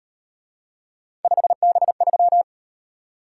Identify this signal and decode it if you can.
Morse code, heard as a keyed tone, about 38 wpm, 700 Hz. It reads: 5B3